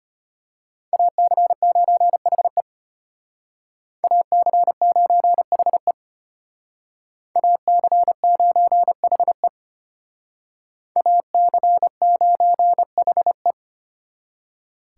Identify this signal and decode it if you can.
Morse code, heard as a keyed tone, about 32 wpm, 700 Hz. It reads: AC9HE AC9HE AC9HE AC9HE